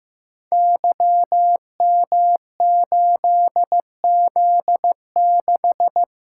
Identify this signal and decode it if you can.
Morse code, heard as a keyed tone, about 15 wpm, 700 Hz. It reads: YM8Z6
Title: Code Groups - 5 Letter:Numbers - 15wpm